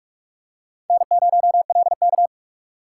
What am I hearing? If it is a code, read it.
Morse code, heard as a keyed tone, about 46 wpm, 700 Hz. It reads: N0LK
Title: USA Call Signs - 4 Characters - 6x Character Spacing - 45wpm